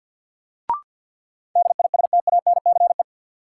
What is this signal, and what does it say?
Morse code, heard as a keyed tone, about 49 wpm, 700 Hz. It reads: DISTANCE
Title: Sentences from Top 1200 Words - New Word Drill Alt - 50wpm